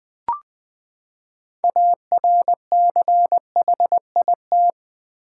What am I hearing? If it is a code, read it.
Morse code, heard as a keyed tone, about 20 wpm, 700 Hz. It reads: ARCHIT
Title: Build A Word - 6 Letters At A Time - 12-Letter Words - 20wpm